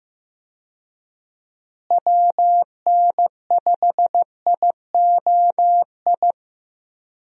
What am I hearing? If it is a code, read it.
Morse code, heard as a keyed tone, about 15 wpm, 700 Hz. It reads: WN5IOI
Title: USA Call Signs - 6 Characters - Repeated 2x - 15wpm